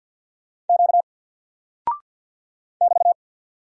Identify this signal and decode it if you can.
Morse code, heard as a keyed tone, about 50 wpm, 700 Hz. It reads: Q <BT>
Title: Single Character (Letter:Number:Punctuation:Prosign) 50wpm